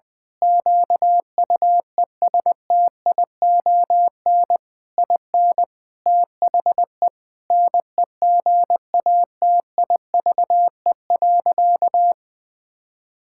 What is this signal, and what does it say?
Morse code, heard as a keyed tone, about 20 wpm, 700 Hz. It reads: QUESTION IN THE NEGATIVE.
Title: Sentences from Top 1000 Words - Review Part 6 - 20wpm